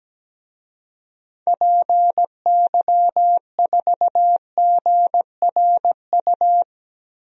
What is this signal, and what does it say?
Morse code, heard as a keyed tone, about 17 wpm, 700 Hz. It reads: PY4GRU
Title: DXCC-100 Call Signs - 6 Characters - Repeated 2x - 17wpm